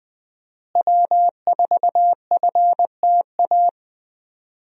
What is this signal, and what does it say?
Morse code, heard as a keyed tone, about 20 wpm, 700 Hz. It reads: W4FTA
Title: USA Call Signs - 5 Characters - Repeated 2x - 20wpm